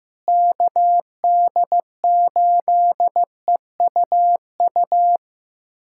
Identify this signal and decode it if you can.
Morse code, heard as a keyed tone, about 15 wpm, 700 Hz. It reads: KD8EUU
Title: USA Callsign DE Callsign - 6 Characters - 15wpm